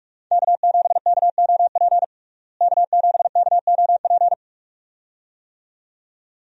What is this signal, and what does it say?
Morse code, heard as a keyed tone, about 41 wpm, 700 Hz. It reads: K7KOP K7KOP
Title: USA Call Signs - 5 Characters - Repeated 2x - 45wpm